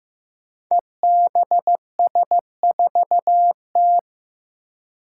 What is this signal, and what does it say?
Morse code, heard as a keyed tone, about 15 wpm, 700 Hz. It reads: EBS4T
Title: Five Challenging Letter-Number Combinations 15wpm